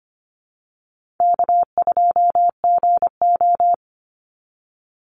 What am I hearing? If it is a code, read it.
Morse code, heard as a keyed tone, about 25 wpm, 700 Hz. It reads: K2GO
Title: Easy Call Signs - 25wpm